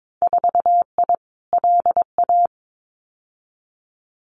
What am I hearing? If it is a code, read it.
Morse code, heard as a keyed tone, about 22 wpm, 700 Hz. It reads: 4I LA
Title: Winter Field Day Exchanges Encore 22wpm